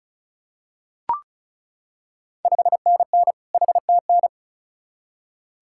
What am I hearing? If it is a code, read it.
Morse code, heard as a keyed tone, about 36 wpm, 690 Hz. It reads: 5NN HTN
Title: Ontario QSO Party Exchange 35wpm